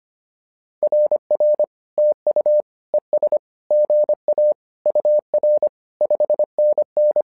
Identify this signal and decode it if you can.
Morse code, heard as a keyed tone, about 25 wpm, 600 Hz. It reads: RR TU ES GA UR 5NN
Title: SOTA QSO Practice 25wpm